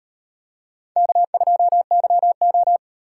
Morse code, heard as a keyed tone, about 38 wpm, 700 Hz. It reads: K2YO